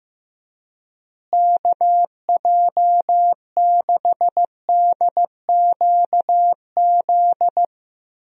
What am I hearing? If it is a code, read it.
Morse code, heard as a keyed tone, about 15 wpm, 700 Hz. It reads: KJ6DQZ
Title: USA Call Signs - 6 Characters - Repeated 2x - 15wpm